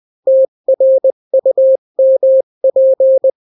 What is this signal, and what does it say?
Morse code, heard as a keyed tone, about 20 wpm, 530 Hz. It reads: TRUMP
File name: daily / 18wpm